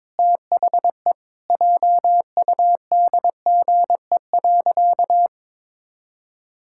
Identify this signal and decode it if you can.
Morse code, heard as a keyed tone, about 22 wpm, 700 Hz. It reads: THE JUDGE.
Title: Pangram Challenge v2 - 22wpm